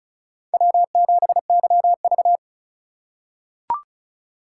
Morse code, heard as a keyed tone, about 36 wpm, 700 Hz. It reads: W7YV
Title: USA Call Signs - 4 Characters - 3x Character Spacing - 35wpm